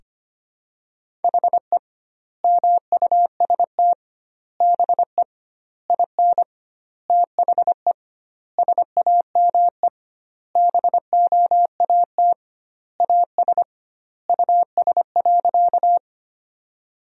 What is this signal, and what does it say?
Morse code, heard as a keyed tone, about 25 wpm, 700 Hz. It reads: HE MUST BE IN THE SAME BOAT AS US.
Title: 2x Word-Spacing - Sentences from Top 500 Words - 25wpm